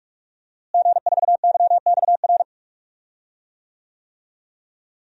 Morse code, heard as a keyed tone, about 44 wpm, 700 Hz. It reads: G4YXR
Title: DXCC-100 Call Signs - 5 Characters - Repeated 2x - 45wpm